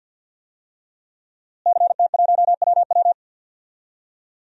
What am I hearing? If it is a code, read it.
Morse code, heard as a keyed tone, about 51 wpm, 700 Hz. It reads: CT1WW
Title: DXCC-100 Call Signs - 5 Characters - Repeated 3x - 50wpm